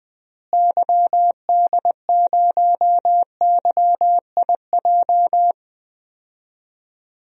Morse code, heard as a keyed tone, about 21 wpm, 700 Hz. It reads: YD0YIJ
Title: DXCC-100 Call Signs - 6 Characters - Repeated 3x - 20wpm